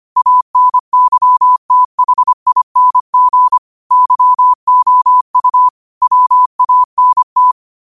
Morse code, heard as a keyed tone, about 25 wpm, 1000 Hz. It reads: ANYTHING YOU WANT
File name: qotdfiles / 25wpm